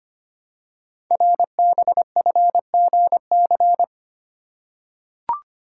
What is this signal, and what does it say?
Morse code, heard as a keyed tone, about 25 wpm, 700 Hz. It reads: RBFGC
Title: Code Groups - 5 Letters - 25wpm